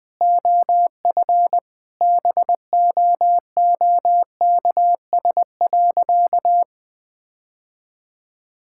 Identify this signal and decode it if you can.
Morse code, heard as a keyed tone, about 20 wpm, 700 Hz. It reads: OF BOOKS.
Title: Sentences from Top 1000 Words - Review Part 5 - 20wpm